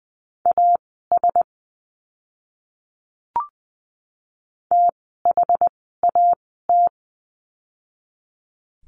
Morse code, heard as a keyed tone, about 20 wpm, spaced as for 14 wpm, 700 Hz. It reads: AS THAT
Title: 115 Lesson 19f - Letter w - Words 20wpm